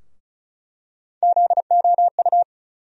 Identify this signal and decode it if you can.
Morse code, heard as a keyed tone, about 36 wpm, 700 Hz. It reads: ZOU
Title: USA Call Sign Postfixes - 3 Letters 35wpm